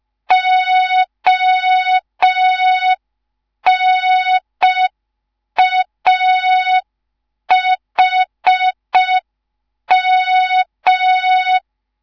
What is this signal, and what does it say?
Morse code, heard as a keyed tone, about 5 wpm, 3700 Hz. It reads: ONAHM